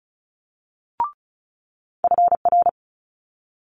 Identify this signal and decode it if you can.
Morse code, heard as a keyed tone, about 36 wpm, 700 Hz. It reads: FR
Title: Multi-letter Phonemes Quick Fire 35wpm